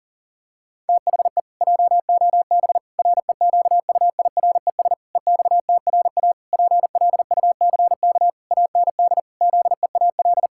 Morse code, heard as a keyed tone, about 40 wpm, 700 Hz. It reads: THE JOB REQUIRES EXTRA PLUCK AND ZEAL
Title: Pangram Challenge v2 - 40wpm